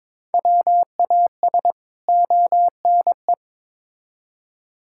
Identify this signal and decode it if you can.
Morse code, heard as a keyed tone, about 22 wpm, 700 Hz. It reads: WAS ONE
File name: practice / 22wpm / Build A Phrase - 5-Word Phrases - 22wpm